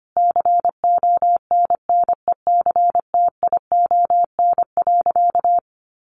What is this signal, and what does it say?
Morse code, heard as a keyed tone, about 25 wpm, 700 Hz. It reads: CONNECTION.
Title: Sentences from Top 1300 Words 25wpm